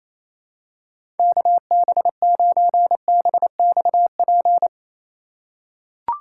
Morse code, heard as a keyed tone, about 28 wpm, 700 Hz. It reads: KB9BXP
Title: USA Call Signs - 6 Characters - 4x Character Spacing - 28wpm